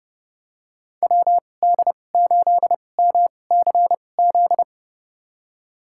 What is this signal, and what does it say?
Morse code, heard as a keyed tone, about 30 wpm, spaced as for 22 wpm, 700 Hz. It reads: WD8MCZ
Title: USA Call Signs - 6 Characters - 2x Character Spacing - 30wpm